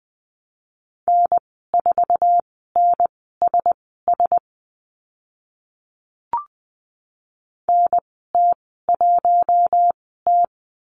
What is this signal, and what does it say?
Morse code, heard as a keyed tone, about 20 wpm, spaced as for 14 wpm, 700 Hz. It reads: N4NSS NT1T
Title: 048 Lesson 9g - Number 4 - Call Signs 20wpm